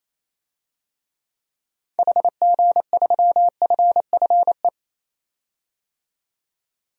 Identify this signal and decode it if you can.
Morse code, heard as a keyed tone, about 28 wpm, 700 Hz. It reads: HG3FFE
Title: DXCC-100 Call Signs - 6 Characters - 28wpm